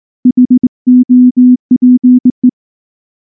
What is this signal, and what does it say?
Morse code, heard as a keyed tone, about 20 wpm, 250 Hz. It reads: HOPE
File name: morse / morse-training-20WPM-9